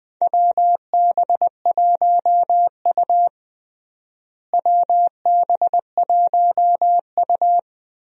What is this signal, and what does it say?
Morse code, heard as a keyed tone, about 20 wpm, 690 Hz. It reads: WB1U WB1U
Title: USA Call Signs - 4 Characters - Repeated 3x - 20wpm